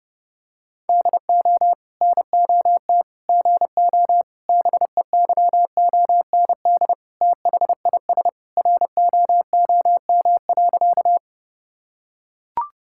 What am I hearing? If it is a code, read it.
Morse code, heard as a keyed tone, about 30 wpm, 700 Hz. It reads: DO NOT GO BEYOND THIS ROOM.
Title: Sentences from Top 1000 Words - Review Part 6 - 30wpm